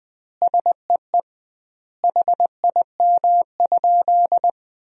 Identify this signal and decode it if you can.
Morse code, heard as a keyed tone, about 21 wpm, 700 Hz. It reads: SEE HIM?
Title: 2x Word-Spacing - Sentences from Top 100 Words - 20wpm